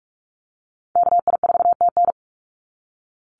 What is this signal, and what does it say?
Morse code, heard as a keyed tone, about 46 wpm, 700 Hz. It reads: KI4TN
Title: NA SOTA Activator Callsigns 45wpm